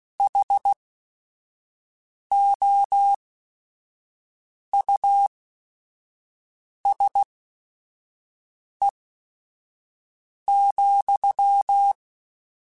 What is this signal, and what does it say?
Morse code, heard as a keyed tone, about 16 wpm, spaced as for 5 wpm, 800 Hz. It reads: HOUSE,
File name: General / General-mp3s / Houses @5